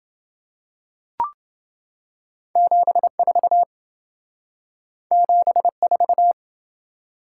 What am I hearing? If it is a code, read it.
Morse code, heard as a keyed tone, about 27 wpm, 700 Hz. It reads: 74 74